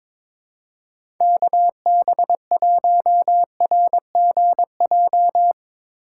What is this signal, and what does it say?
Morse code, heard as a keyed tone, about 22 wpm, 700 Hz. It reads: KB1RGJ
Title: USA Call Signs - 6 Characters - 8x Character Spacing - 22wpm